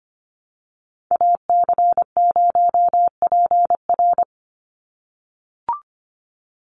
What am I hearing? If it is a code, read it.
Morse code, heard as a keyed tone, about 25 wpm, 700 Hz. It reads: AC0PR
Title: NA SOTA Chaser Callsigns 25wpm